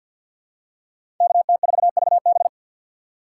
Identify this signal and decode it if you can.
Morse code, heard as a keyed tone, about 51 wpm, 700 Hz. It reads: KT4VB
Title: USA Call Signs - 6x Character Spacing - 50wpm